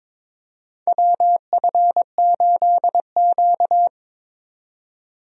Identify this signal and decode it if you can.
Morse code, heard as a keyed tone, about 22 wpm, 700 Hz. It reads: WF8Q
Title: USA Call Signs - 4 Characters - 22wpm